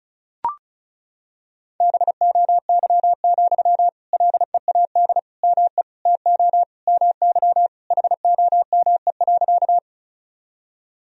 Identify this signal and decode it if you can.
Morse code, heard as a keyed tone, about 36 wpm, 700 Hz. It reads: BOY, LEAD ME TO MY HOME.
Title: Sentences from Top 1000 Words - Review Part 2 - 35wpm